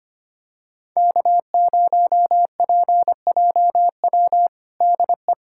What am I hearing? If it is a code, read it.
Morse code, heard as a keyed tone, about 25 wpm, 700 Hz. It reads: K0PJW DE
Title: USA Callsign DE Callsign - 5 Characters - 25wpm